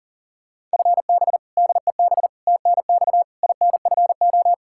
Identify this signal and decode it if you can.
Morse code, heard as a keyed tone, about 40 wpm, 700 Hz. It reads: FB DEB TNX INFO
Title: K1USN SST QSO Practice 40wpm